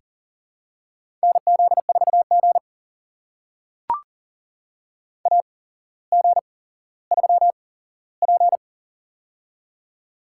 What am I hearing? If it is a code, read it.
Morse code, heard as a keyed tone, about 40 wpm, spaced as for 17 wpm, 700 Hz. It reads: NZ4G AG3P
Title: USA Call Signs - 4 Characters - 8x Character Spacing - 40wpm